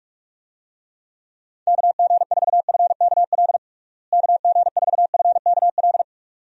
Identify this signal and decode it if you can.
Morse code, heard as a keyed tone, about 41 wpm, 700 Hz. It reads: KG4FKL KG4FKL
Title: USA Call Signs - 6 Characters - Repeated 2x - 45wpm